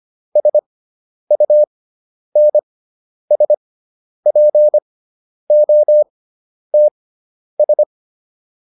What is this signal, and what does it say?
Morse code, heard as a keyed tone, about 25 wpm, spaced as for 10 wpm, 600 Hz. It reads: SUNSPOTS